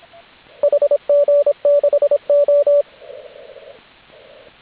Morse code, heard as a keyed tone, about 26 wpm, 580 Hz, with radio noise. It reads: HG6O
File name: AUDIO2019 / HG8DX CW